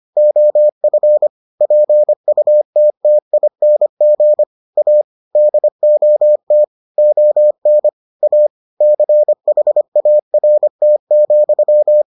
Morse code, heard as a keyed tone, about 25 wpm, 600 Hz. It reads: OF PUTTING A DOT ON A CHART,